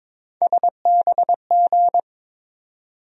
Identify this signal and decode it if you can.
Morse code, heard as a keyed tone, about 22 wpm, 700 Hz. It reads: SBG
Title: USA Call Sign Postfixes - 3 Letters 22wpm